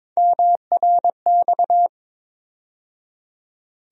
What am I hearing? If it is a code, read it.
Morse code, heard as a keyed tone, about 22 wpm, 700 Hz. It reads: MRX
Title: USA Call Sign Postfixes - 3 Letters 22wpm